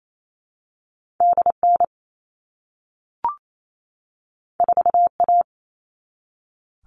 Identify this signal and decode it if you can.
Morse code, heard as a keyed tone, about 28 wpm, 700 Hz. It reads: DN 4A
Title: All Cut Numbers - 2-Digit 28wpm